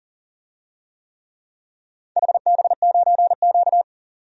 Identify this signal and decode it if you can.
Morse code, heard as a keyed tone, about 40 wpm, 700 Hz. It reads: HB9Q
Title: DXCC-100 Call Signs - 4 Characters - 40wpm